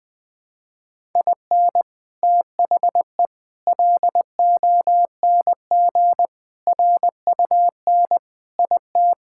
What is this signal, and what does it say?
Morse code, heard as a keyed tone, about 20 wpm, 700 Hz. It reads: IN THE LONG RUN IT
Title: Sentences with 4-Word Phrases - 20wpm